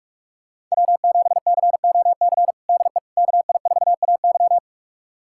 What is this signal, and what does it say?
Morse code, heard as a keyed tone, about 46 wpm, 700 Hz. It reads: W7COC DE KI4AY
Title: USA Callsign DE Callsign - 5 Characters - 45wpm